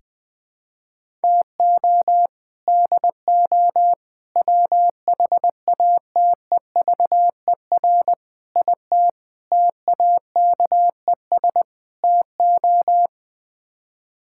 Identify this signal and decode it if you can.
Morse code, heard as a keyed tone, about 21 wpm, 700 Hz. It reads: TO DO WHATEVER IT TAKES TO
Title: Sets of 6 Words - 20wpm